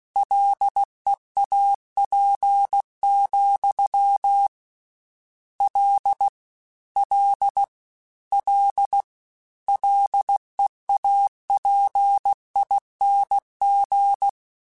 Morse code, heard as a keyed tone, about 16 wpm, 800 Hz. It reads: LEAP, L L L LEAPING
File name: Combinations / Combinations-mp3s / Combinations@15